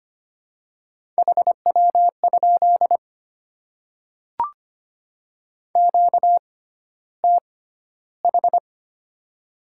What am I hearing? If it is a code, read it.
Morse code, heard as a keyed tone, about 25 wpm, spaced as for 13 wpm, 700 Hz. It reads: HW? QTH
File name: Top 20 QSO Elements - 6x Character Spacing - 25wpm